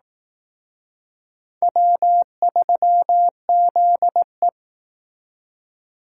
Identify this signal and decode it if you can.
Morse code, heard as a keyed tone, about 18 wpm, 700 Hz. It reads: W3ZE
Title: Speed Racing - USA 1x2 Call Signs 15wpm